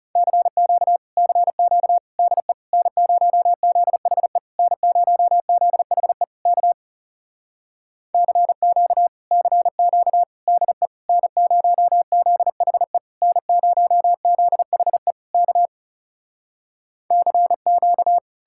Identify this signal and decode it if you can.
Morse code, heard as a keyed tone, about 40 wpm, 700 Hz. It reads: CQ CQ DE N0ZHE N0ZHE K CQ CQ DE N0ZHE N0ZHE K CQ